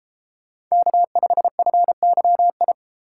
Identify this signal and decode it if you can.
Morse code, heard as a keyed tone, about 32 wpm, 700 Hz. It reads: K5FYI